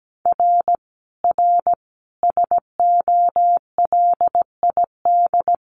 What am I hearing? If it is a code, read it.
Morse code, heard as a keyed tone, about 17 wpm, 700 Hz. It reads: R R SOLID
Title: Common QSO Phrases 17wpm